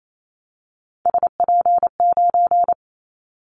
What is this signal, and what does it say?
Morse code, heard as a keyed tone, about 28 wpm, 700 Hz. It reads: SP9
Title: Callsign Prefixes for Top 25 DXCC - 28wpm